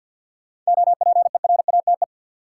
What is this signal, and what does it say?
Morse code, heard as a keyed tone, about 47 wpm, 700 Hz. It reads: OPERATE
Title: Sentences from Top 1500 Words - New Word Drill Alt - 50wpm